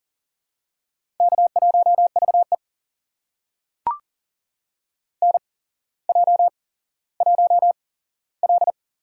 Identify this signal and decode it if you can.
Morse code, heard as a keyed tone, about 38 wpm, spaced as for 17 wpm, 700 Hz. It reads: K1VE NJ1L
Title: USA Call Signs - 4 Characters - 8x Character Spacing - 40wpm